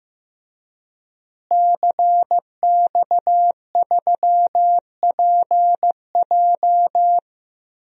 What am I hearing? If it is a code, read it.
Morse code, heard as a keyed tone, about 16 wpm, 700 Hz. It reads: CX3PJ
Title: DXCC-100 Call Signs - 5 Characters - Repeated 3x - 15wpm